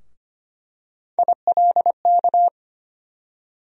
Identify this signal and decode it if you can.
Morse code, heard as a keyed tone, about 25 wpm, 700 Hz. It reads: ILK
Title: USA Call Sign Postfixes - 3 Letters 25wpm